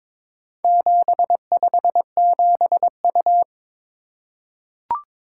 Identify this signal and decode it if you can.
Morse code, heard as a keyed tone, about 22 wpm, 700 Hz. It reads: 757U
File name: practice / 22wpm / Four Letter-Number Combinations - 22wpm